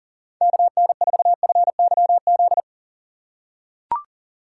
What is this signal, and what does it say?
Morse code, heard as a keyed tone, about 39 wpm, 700 Hz. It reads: KN4FYZ
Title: USA Call Signs - 6 Characters - 6x Character Spacing - 40wpm